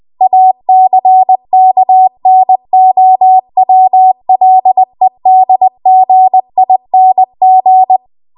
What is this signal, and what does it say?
Morse code, heard as a keyed tone, about 20 wpm, 750 Hz. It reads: ACKNOWLEDGING
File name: Morse / 20WPM